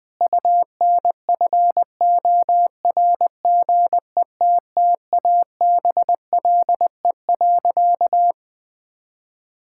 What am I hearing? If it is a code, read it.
Morse code, heard as a keyed tone, about 20 wpm, 700 Hz. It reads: UNFORGETTABLE.
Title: Sentences with 6-Word Phrases - 20wpm